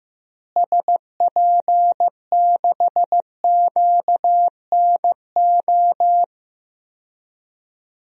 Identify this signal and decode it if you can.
Morse code, heard as a keyed tone, about 16 wpm, 700 Hz. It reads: SP6QNO
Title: DXCC-100 Call Signs - 6 Characters - Repeated 3x - 15wpm